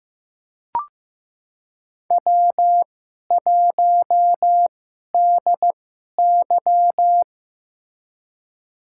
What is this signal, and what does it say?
Morse code, heard as a keyed tone, about 15 wpm, spaced as for 11 wpm, 700 Hz. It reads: W1DY
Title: USA Call Signs - 4 Characters - 2x Character Spacing - 15wpm